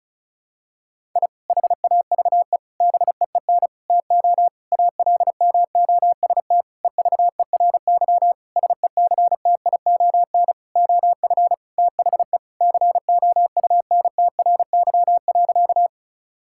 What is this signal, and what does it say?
Morse code, heard as a keyed tone, about 36 wpm, 700 Hz. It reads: I HAVE BEEN TO ALMOST EVERY SECTION OF THE COUNTRY.
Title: Sentences from Top 1000 Words - Review Part 5 - 35wpm